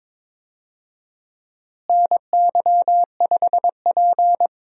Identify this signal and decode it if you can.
Morse code, heard as a keyed tone, about 21 wpm, 700 Hz. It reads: NY5P
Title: USA Call Signs - 4 Characters - Repeated 2x - 22wpm